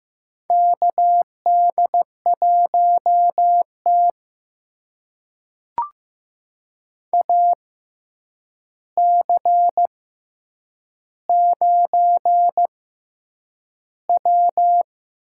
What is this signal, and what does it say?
Morse code, heard as a keyed tone, about 15 wpm, spaced as for 8 wpm, 700 Hz. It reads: KD1T AC9W
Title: USA Call Signs - 4 Characters - 6x Character Spacing - 15wpm